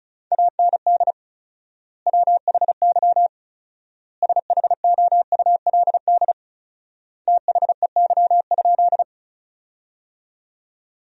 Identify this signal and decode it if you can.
Morse code, heard as a keyed tone, about 34 wpm, 700 Hz. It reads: AND WHY SHOULD THEY?
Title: 4x Word-Spacing - Sentences from Top 300 Words - 35wpm